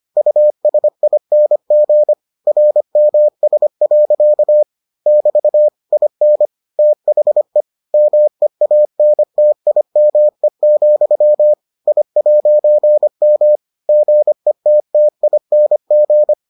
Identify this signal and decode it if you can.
Morse code, heard as a keyed tone, about 25 wpm, 600 Hz. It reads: USING RMS. <BT> IN THE MEANTIME, I'M GETTING